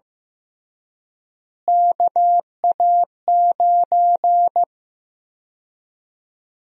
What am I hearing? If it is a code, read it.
Morse code, heard as a keyed tone, about 15 wpm, 700 Hz. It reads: KA9
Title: USA Call Sign Prefixes - Repeated 2x - 15wpm